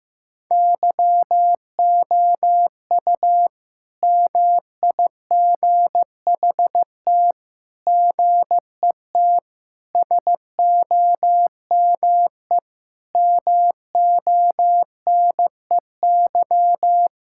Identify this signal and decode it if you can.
Morse code, heard as a keyed tone, about 16 wpm, 700 Hz. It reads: YOU MIGHT GET SOME MONEY
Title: Sentences from Top 1000 Words - Review Part 5 - 15wpm